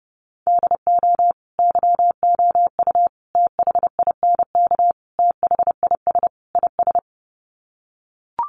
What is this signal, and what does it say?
Morse code, heard as a keyed tone, about 30 wpm, 700 Hz. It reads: DO YOU THINK THIS IS
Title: Sets of 5 Words Encore2 30wpm